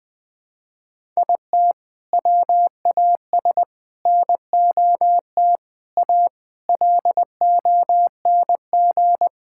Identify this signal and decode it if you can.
Morse code, heard as a keyed tone, about 20 wpm, 700 Hz. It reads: IT WAS NOT A LONG
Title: Sentences from Top 1000 Words - Review Part 6 - 20wpm